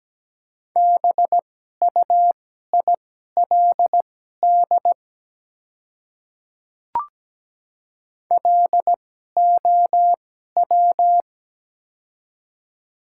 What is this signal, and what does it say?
Morse code, heard as a keyed tone, about 17 wpm, spaced as for 12 wpm, 700 Hz. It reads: BUILD LOW
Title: 2x Character Spacing - Top 200 Words - 17wpm